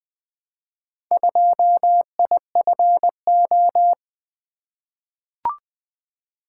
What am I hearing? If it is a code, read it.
Morse code, heard as a keyed tone, about 20 wpm, 700 Hz. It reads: 2IFO
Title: Four Letter-Number Combinations - 20wpm